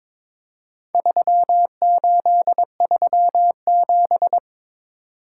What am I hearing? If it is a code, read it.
Morse code, heard as a keyed tone, about 22 wpm, 700 Hz. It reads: 3837
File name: practice / 22wpm / Four Letter-Number Combinations - 22wpm